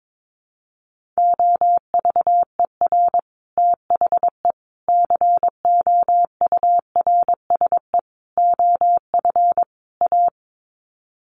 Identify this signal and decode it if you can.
Morse code, heard as a keyed tone, about 22 wpm, 700 Hz. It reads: OVER THE COURSE OF A
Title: Sets of 5 Words Encore 22wpm